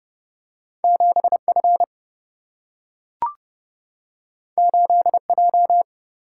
Morse code, heard as a keyed tone, about 30 wpm, 700 Hz. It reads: 7 F 8 J
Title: Single-Letter Single-Number 30wpm